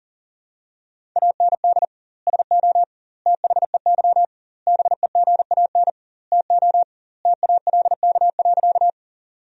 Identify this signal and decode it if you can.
Morse code, heard as a keyed tone, about 40 wpm, 700 Hz. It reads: AND SO THEY BEGAN TO TALK.
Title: 2x Word-Spacing - Sentences from Top 400 Words - 40wpm